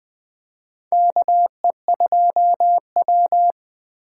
Morse code, heard as a keyed tone, about 21 wpm, 700 Hz. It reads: KE2W
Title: USA Call Signs - 4 Characters - 4x Character Spacing - 20wpm